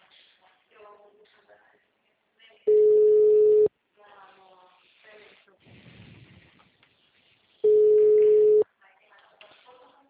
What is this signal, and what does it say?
Ringback tone (Europe, 425 Hz), with speech.